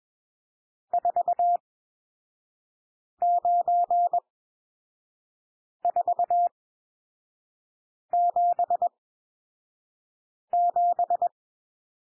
Morse code, heard as a keyed tone, about 21 wpm, spaced as for 5 wpm, 700 Hz. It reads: 49477
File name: cwmp3 / N1 SOUND NUMBERS 21 wpm